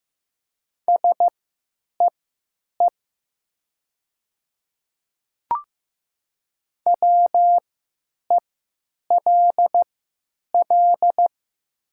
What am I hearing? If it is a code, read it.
Morse code, heard as a keyed tone, about 15 wpm, spaced as for 9 wpm, 700 Hz. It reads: SEE WELL
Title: 3x Character Spacing - Top 300 Words - 15wpm